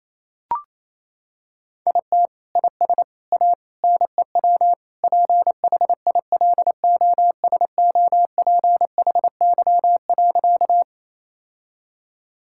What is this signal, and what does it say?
Morse code, heard as a keyed tone, about 28 wpm, 700 Hz. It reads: IT IS A NEW PHILOSOPHY.
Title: Sentences from Top 1000 Words - Review Part 6 - 28wpm